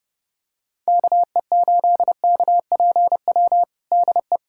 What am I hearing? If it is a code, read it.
Morse code, heard as a keyed tone, about 30 wpm, 700 Hz. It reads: KE8KPW DE
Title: USA Callsign DE Callsign 30wpm